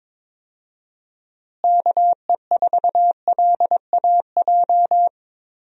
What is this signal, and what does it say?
Morse code, heard as a keyed tone, about 22 wpm, 700 Hz. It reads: KE4LAJ
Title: USA Call Signs - 6 Characters - Repeated 3x - 22wpm